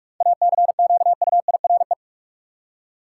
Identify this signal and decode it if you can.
Morse code, heard as a keyed tone, about 46 wpm, 700 Hz. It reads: ACQUIRE
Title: Seven Letter Words - 45wpm